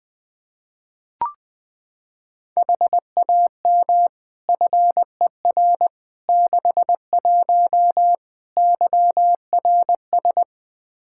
Morse code, heard as a keyed tone, about 21 wpm, 700 Hz. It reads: HAM FER 61 YRS
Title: Practice QSO - Exchange 3 - More Info - 20wpm